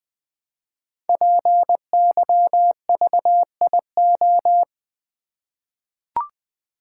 Morse code, heard as a keyed tone, about 20 wpm, 700 Hz. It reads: PYVIO
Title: Code Groups - 5 Letter:Numbers - 20wpm